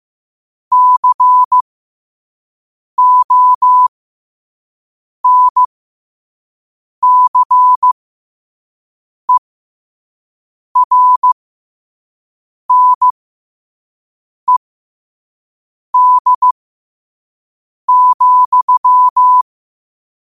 Morse code, heard as a keyed tone, about 15 wpm, spaced as for 5 wpm, 1000 Hz. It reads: CONCERNED,
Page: New quotes every day in morse code at 5 Words per minute.